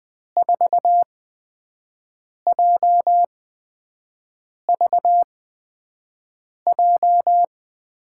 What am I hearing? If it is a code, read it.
Morse code, heard as a keyed tone, about 20 wpm, spaced as for 5 wpm, 700 Hz. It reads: 4JVJ
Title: USA Call Signs - 8x Character Spacing - 20wpm